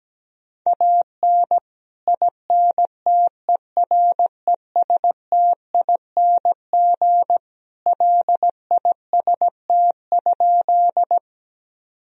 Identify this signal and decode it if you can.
Morse code, heard as a keyed tone, about 17 wpm, 700 Hz. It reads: AN INTERESTING LIST?
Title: Sentences from Top 1000 Words - Review Part 2 - 17wpm